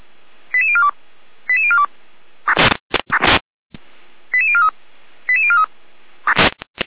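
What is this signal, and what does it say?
Five-tone selective call (ZVEI), heard as quick runs of five tones, tones of about 70 ms. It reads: 88042 88042 88043 88043
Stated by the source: Received on an SDR as a ZVEI five-tone call.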